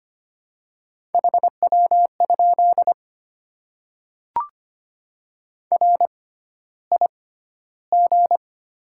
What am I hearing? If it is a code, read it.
Morse code, heard as a keyed tone, about 25 wpm, spaced as for 13 wpm, 700 Hz. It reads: HW? RIG
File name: Top 20 QSO Elements - 6x Character Spacing - 25wpm